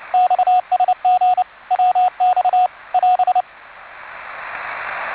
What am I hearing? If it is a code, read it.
Morse code, heard as a keyed tone, about 29 wpm, 730 Hz, with radio noise. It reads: XSG WX <AS>